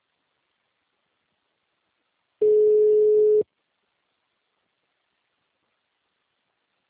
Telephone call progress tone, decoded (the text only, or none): ringback tone